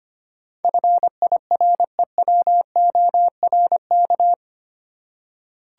Morse code, heard as a keyed tone, about 25 wpm, 700 Hz. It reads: FIREWORK
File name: 8 Letter Words - First 3 Letters Given - 25wpm